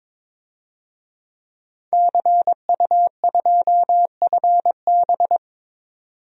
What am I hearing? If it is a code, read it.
Morse code, heard as a keyed tone, about 22 wpm, 700 Hz. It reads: CU2FB
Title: DXCC-100 Call Signs - 5 Characters - 22wpm